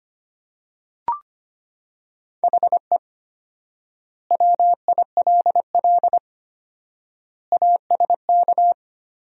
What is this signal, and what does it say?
Morse code, heard as a keyed tone, about 25 wpm, 700 Hz. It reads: HE WILL ASK
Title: 4x Word-Spacing - Sentences from Top 200 Words - 25wpm